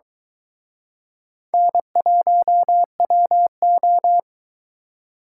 Morse code, heard as a keyed tone, about 21 wpm, 700 Hz. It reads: N1WO
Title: Speed Racing - USA 1x2 Call Signs 17wpm